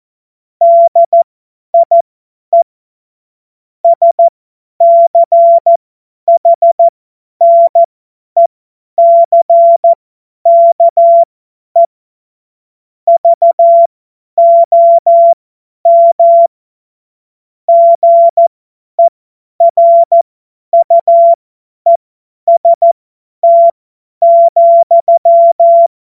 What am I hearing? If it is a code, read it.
Morse code, heard as a keyed tone, about 14 wpm, spaced as for 10 wpm, 680 Hz. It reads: DIE SCHNECKE VOM GERUEST,
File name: Morsekurs / Witze / Witz 31.Oktober 2023 50 F70